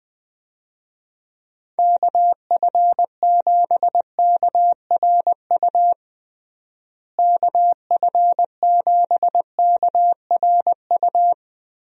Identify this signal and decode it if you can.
Morse code, heard as a keyed tone, about 21 wpm, 700 Hz. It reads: KF7KRU KF7KRU
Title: USA Call Signs - 6 Characters - Repeated 3x - 20wpm